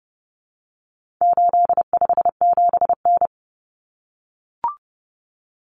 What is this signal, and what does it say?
Morse code, heard as a keyed tone, about 30 wpm, 700 Hz. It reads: 857N
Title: Common Cut Numbers - 4-Digit 30wpm